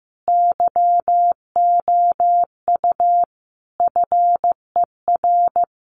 Morse code, heard as a keyed tone, about 15 wpm, 700 Hz. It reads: YOU FER
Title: Common QSO Phrases 15wpm